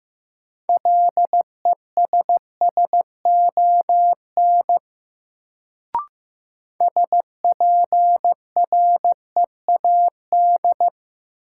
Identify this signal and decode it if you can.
Morse code, heard as a keyed tone, about 15 wpm, 700 Hz. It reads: LESSON SPREAD
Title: Sentences from Top 1200 Words - New Word Drill Alt - 15wpm